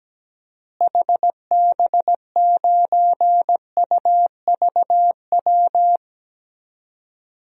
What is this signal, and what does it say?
Morse code, heard as a keyed tone, about 17 wpm, 700 Hz. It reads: HB9UVW
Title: DXCC-100 Call Signs - 6 Characters - Repeated 2x - 17wpm